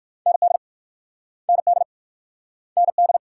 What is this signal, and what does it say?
Morse code, heard as a keyed tone, about 56 wpm, 690 Hz. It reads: ND ND ND
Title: Speed-Racing Multi-letter Phonemes - 45wpm